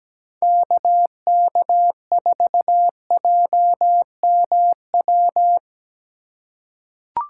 Morse code, heard as a keyed tone, about 17 wpm, 700 Hz. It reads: KK4JMW
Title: USA Call Signs - 6 Characters - 3x Character Spacing - 17wpm